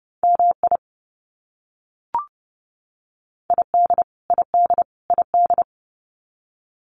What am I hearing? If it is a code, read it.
Morse code, heard as a keyed tone, about 30 wpm, 700 Hz. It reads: MI ID ID ID
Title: US States Repeated 3x 30wpm